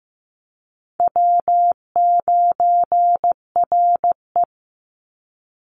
Morse code, heard as a keyed tone, about 15 wpm, 700 Hz. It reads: W9RE
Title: Easy Call Signs Repeated 3x - 15wpm